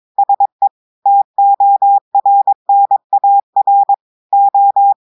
Morse code, heard as a keyed tone, about 22 wpm, 800 Hz. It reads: SE TORNAR O
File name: audios / portugais / portugais texte en clair 2 vitesse 22 mots minutes